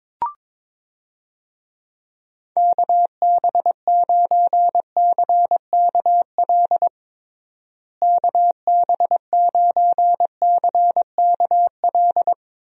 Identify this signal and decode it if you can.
Morse code, heard as a keyed tone, about 22 wpm, 700 Hz. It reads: KB9CKL KB9CKL
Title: USA Call Signs - 6 Characters - Repeated 2x - 22wpm